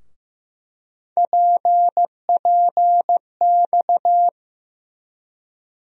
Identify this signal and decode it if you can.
Morse code, heard as a keyed tone, about 15 wpm, 700 Hz. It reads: PPX
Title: USA Call Sign Postfixes - 3 Letters 15wpm